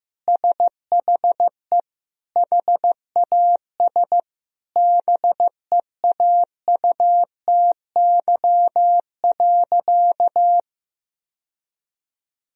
Morse code, heard as a keyed tone, about 16 wpm, 700 Hz. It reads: SHE HAS BEAUTY.
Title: Sentences from Top 1000 Words - Review Part 2 - 15wpm